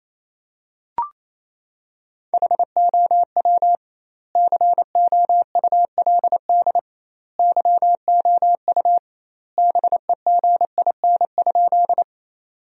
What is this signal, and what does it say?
Morse code, heard as a keyed tone, about 28 wpm, 700 Hz. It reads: HOW COULD YOU BEGIN?
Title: 2x Word-Spacing - Sentences from Top 300 Words - 28wpm